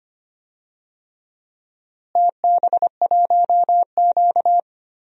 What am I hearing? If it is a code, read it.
Morse code, heard as a keyed tone, about 25 wpm, 700 Hz. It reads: TB1Q
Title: DXCC-100 Call Signs - 4 Characters - Repeated 3x - 25wpm